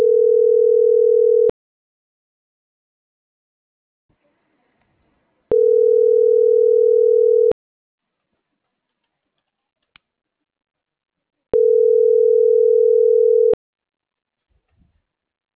Ringback tone (North America), 440 + 480 Hz.